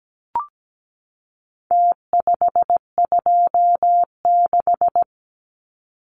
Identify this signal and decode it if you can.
Morse code, heard as a keyed tone, about 17 wpm, 700 Hz. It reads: T526
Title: Numbers - Contest Style - 4-Digit 17wpm